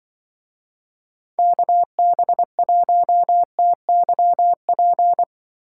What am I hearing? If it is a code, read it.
Morse code, heard as a keyed tone, about 24 wpm, 700 Hz. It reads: KB1TYP
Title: Speed Racing - USA 2x3 Call Signs 20wpm